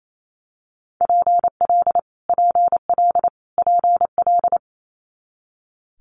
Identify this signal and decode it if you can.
Morse code, heard as a keyed tone, about 28 wpm, 700 Hz. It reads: PL PL PL
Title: Multi-letter Phonemes Repeated 3x 28wpm